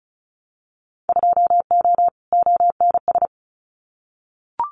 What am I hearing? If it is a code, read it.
Morse code, heard as a keyed tone, about 34 wpm, 700 Hz. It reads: 2O ONS
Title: Winter Field Day Exchanges Encore 35wpm